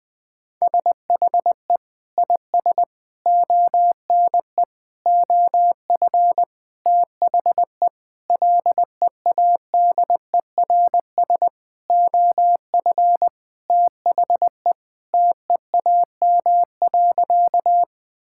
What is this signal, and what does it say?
Morse code, heard as a keyed tone, about 20 wpm, 700 Hz. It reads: SHE IS ONE OF THE LEADERS OF THE TEAM.
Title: Sentences with 6-Word Phrases - Spoken Beginning - 20wpm